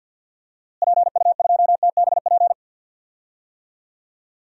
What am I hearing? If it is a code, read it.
Morse code, heard as a keyed tone, about 48 wpm, 700 Hz. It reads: PU2TBP
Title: DXCC-100 Call Signs - 6 Characters - Repeated 2x - 50wpm